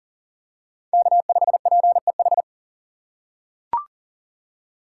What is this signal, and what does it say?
Morse code, heard as a keyed tone, about 40 wpm, 700 Hz. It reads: K5PEH
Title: USA Call Signs - 2x Character Spacing - 40wpm